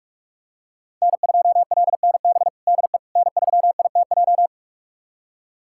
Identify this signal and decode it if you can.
Morse code, heard as a keyed tone, about 46 wpm, 700 Hz. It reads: N2LNB DE N3ITJ
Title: USA Callsign DE Callsign - 5 Characters - 45wpm